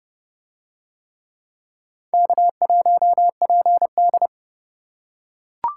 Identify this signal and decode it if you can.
Morse code, heard as a keyed tone, about 29 wpm, 700 Hz. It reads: K1PD